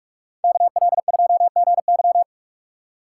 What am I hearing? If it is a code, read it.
Morse code, heard as a keyed tone, about 45 wpm, 700 Hz. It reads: KL2GY